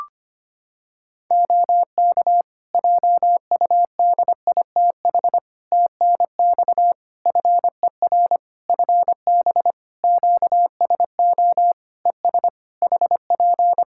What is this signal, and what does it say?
Morse code, heard as a keyed tone, about 25 wpm, 700 Hz. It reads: OK JUDITH TNX FER FB QSO ES HP
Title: Practice QSO - Exchange 4 - Ending - 25wpm